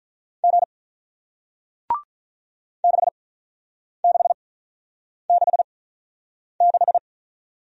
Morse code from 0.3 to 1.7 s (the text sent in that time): G